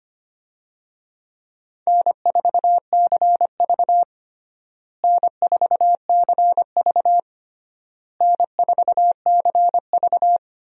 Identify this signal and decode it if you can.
Morse code, heard as a keyed tone, about 25 wpm, 700 Hz. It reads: N4CV N4CV N4CV
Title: USA Call Signs - 4 Characters - Repeated 3x - 25wpm